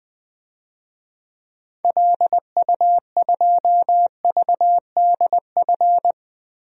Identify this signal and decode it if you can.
Morse code, heard as a keyed tone, about 21 wpm, 700 Hz. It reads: LU2VDF
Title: DXCC-100 Call Signs - 6 Characters - Repeated 3x - 20wpm